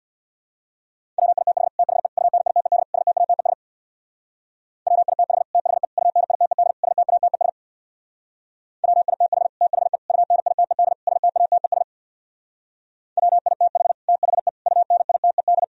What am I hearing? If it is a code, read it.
Morse code, heard as a keyed tone, about 64 wpm, 700 Hz. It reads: WITH THE UNITED STATES WITH THE UNITED STATES WITH THE UNITED STATES WITH THE UNITED